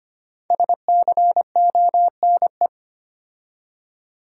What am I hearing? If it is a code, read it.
Morse code, heard as a keyed tone, about 25 wpm, 700 Hz. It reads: SCONE
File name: practice / 25wpm / Difficult Words with Hints - 25wpm